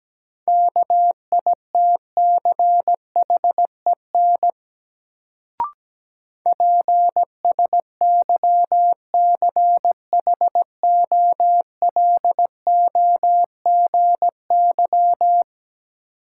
Morse code, heard as a keyed tone, about 17 wpm, 700 Hz. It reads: KITCHEN PSYCHOLOGY
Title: Sentences from Top 1400 Words - New Word Drill - 17wpm